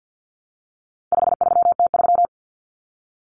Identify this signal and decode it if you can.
Morse code, heard as a keyed tone, about 50 wpm, 700 Hz. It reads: 53T3
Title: Common Cut Numbers - 4-Digit 50wpm